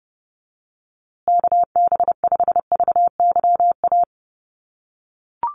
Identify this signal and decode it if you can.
Morse code, heard as a keyed tone, about 30 wpm, 700 Hz. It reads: KB5VYA
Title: USA Call Signs Repeated 2x 30wpm